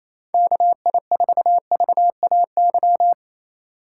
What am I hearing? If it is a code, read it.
Morse code, heard as a keyed tone, about 28 wpm, 700 Hz. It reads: KI4VAY
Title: USA Call Signs - 6 Characters - 2x Character Spacing - 28wpm